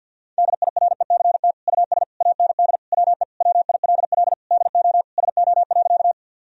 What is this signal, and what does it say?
Morse code, heard as a keyed tone, about 46 wpm, 700 Hz. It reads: DIRECT US AND WE WILL DO SO.
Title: Sentences from Top 1000 Words - Review Part 2 - 50wpm